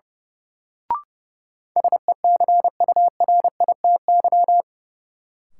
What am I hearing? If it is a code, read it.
Morse code, heard as a keyed tone, about 30 wpm, 700 Hz. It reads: SECURITY
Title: Sentences from Top 1400 Words - New Word Drill Alt - 30wpm